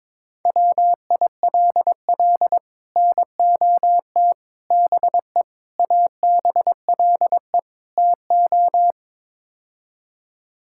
Morse code, heard as a keyed tone, about 22 wpm, 700 Hz. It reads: WILL NOT BE ABLE TO
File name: practice / 22wpm / Build A Phrase - 5-Word Phrases - 22wpm